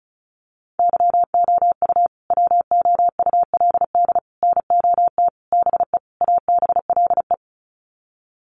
Morse code, heard as a keyed tone, about 34 wpm, 700 Hz. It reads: YOU WOULD NOT BE ABLE
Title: Sets of 5 Words Encore2 35wpm